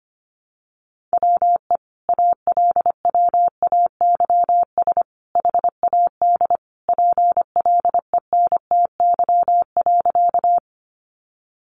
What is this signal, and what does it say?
Morse code, heard as a keyed tone, about 25 wpm, 700 Hz. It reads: WE ALWAYS HAD PLENTY.
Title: Sentences from Top 1300 Words 25wpm